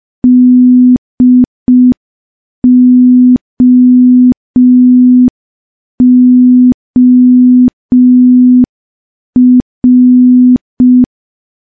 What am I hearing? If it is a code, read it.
Morse code, heard as a keyed tone, about 5 wpm, 250 Hz. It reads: DOOR
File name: morse / morse-training-05WPM